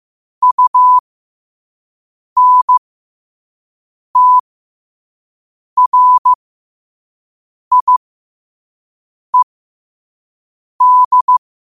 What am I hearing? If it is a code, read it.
Morse code, heard as a keyed tone, about 15 wpm, spaced as for 5 wpm, 1000 Hz. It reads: UNTRIED
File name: qotdfiles / 5wpm